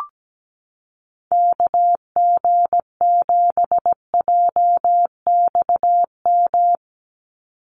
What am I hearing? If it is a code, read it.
Morse code, heard as a keyed tone, about 17 wpm, 700 Hz. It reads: KG7JXM
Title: USA Call Signs 17wpm